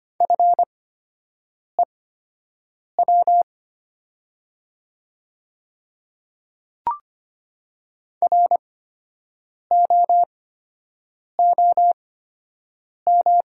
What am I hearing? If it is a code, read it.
Morse code, heard as a keyed tone, about 25 wpm, spaced as for 7 wpm, 690 Hz. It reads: FEW ROOM
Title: 8x Character Spacing - Top 500 Words - 25wpm